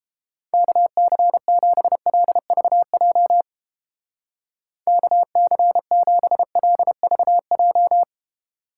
Morse code, heard as a keyed tone, about 29 wpm, 700 Hz. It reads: KC7LVJ KC7LVJ